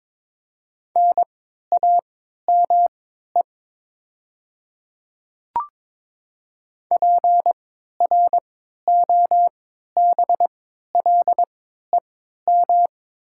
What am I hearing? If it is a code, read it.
Morse code, heard as a keyed tone, about 22 wpm, spaced as for 12 wpm, 700 Hz. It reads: NAME PROBLEM
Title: 3x Character Spacing - Top 500 Words - 22wpm